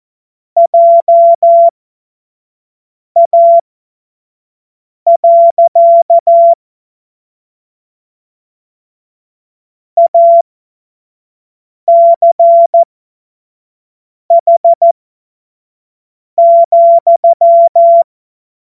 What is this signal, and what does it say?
Morse code, heard as a keyed tone, about 14 wpm, spaced as for 5 wpm, 680 Hz. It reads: JA. ACH,
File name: Morsekurs / Witze / Witz 15. Februar 2022 25 F70